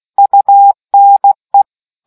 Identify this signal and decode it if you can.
Morse code, heard as a keyed tone, about 16 wpm, 800 Hz. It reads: UNE